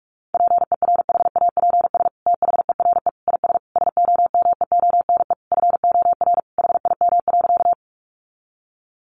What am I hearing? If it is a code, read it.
Morse code, heard as a keyed tone, about 44 wpm, 700 Hz. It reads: PERHAPS THERE IS SOMEONE FOR HIM.
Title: Sentences from Top 600 Words 45wpm